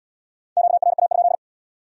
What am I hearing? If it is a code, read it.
Morse code, heard as a keyed tone, about 64 wpm, 700 Hz. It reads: QRA?